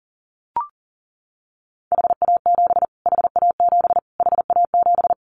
Morse code, heard as a keyed tone, about 40 wpm, 700 Hz. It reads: HA7 HA7 HA7
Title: Callsign Prefixes for Top 25 DXCC - Repeated 3x - 40wpm